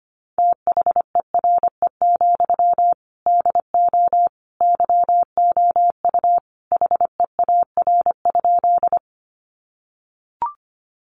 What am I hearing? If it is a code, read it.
Morse code, heard as a keyed tone, about 25 wpm, 700 Hz. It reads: THERE, DO YOU HEAR?
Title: Sentences from Top 300 Words 25wpm